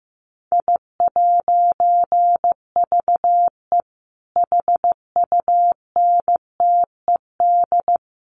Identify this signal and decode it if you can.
Morse code, heard as a keyed tone, about 15 wpm, 700 Hz. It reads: I'VE HUNTED
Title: A Sound Of Thunder 15wpm